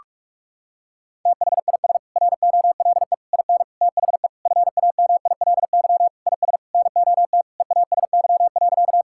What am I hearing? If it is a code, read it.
Morse code, heard as a keyed tone, about 41 wpm, 700 Hz. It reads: THIS ROLE IN THE FAMILY IS NOT EASY.
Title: Sentences from Top 1000 Words - Review Part 6 - 45wpm